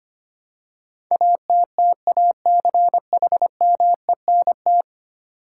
Morse code, heard as a keyed tone, about 25 wpm, 700 Hz. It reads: ATTACHMENT
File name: 10 Letter Words - First 3 Letters Given - 25wpm